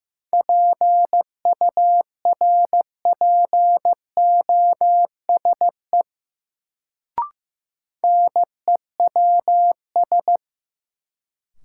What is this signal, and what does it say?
Morse code, heard as a keyed tone, about 16 wpm, 700 Hz. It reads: PURPOSE NEWS
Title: Sets of 5 Words - New Word Drill - 15wpm